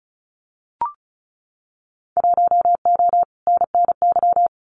Morse code, heard as a keyed tone, about 34 wpm, 700 Hz. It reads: 1O NNY
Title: Winter Field Day Exchanges 35wpm